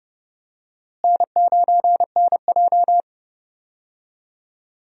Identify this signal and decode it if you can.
Morse code, heard as a keyed tone, about 32 wpm, 700 Hz. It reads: N9NJ
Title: USA Call Signs - 4 Characters - Repeated 2x - 30wpm